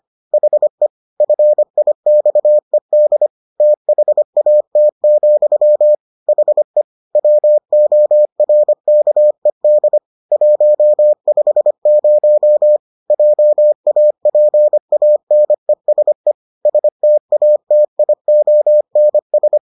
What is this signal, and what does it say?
Morse code, heard as a keyed tone, about 25 wpm, 600 Hz. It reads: HE FIXED THAT, HE WORKED 150 JAPANESE STATIONS